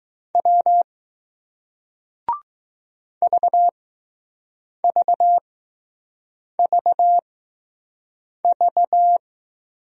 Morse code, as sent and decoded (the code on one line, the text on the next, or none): .-- / ...- ...- ...- ...-
W VVVV